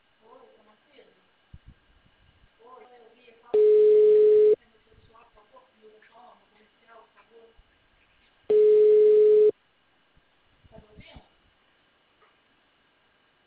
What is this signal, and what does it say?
Ringback tone (Europe, 425 Hz), with speech.